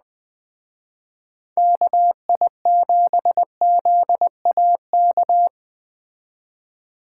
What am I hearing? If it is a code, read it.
Morse code, heard as a keyed tone, about 20 wpm, 700 Hz. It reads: KI7ZAK